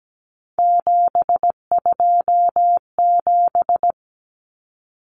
Morse code, heard as a keyed tone, about 17 wpm, 700 Hz. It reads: 727
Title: Numbers - 3-Digit 17wpm